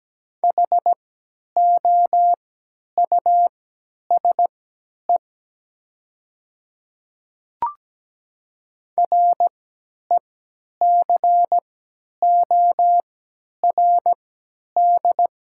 Morse code, heard as a keyed tone, about 17 wpm, spaced as for 10 wpm, 700 Hz. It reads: HOUSE RECORD
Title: 3x Character Spacing - Top 500 Words - 17wpm